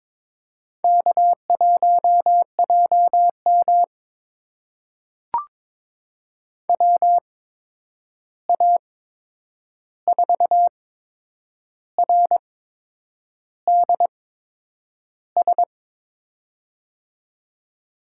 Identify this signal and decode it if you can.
Morse code, heard as a keyed tone, about 22 wpm, spaced as for 8 wpm, 700 Hz. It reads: K1JM WA4RDS
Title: USA Call Signs - 8x Character Spacing - 22wpm